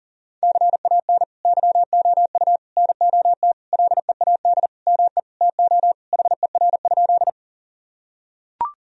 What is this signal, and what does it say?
Morse code, heard as a keyed tone, about 40 wpm, 700 Hz. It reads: CAN YOU NOT LEAD ME TO HER?
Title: Sentences from Top 1000 Words - Review Part 2 - 40wpm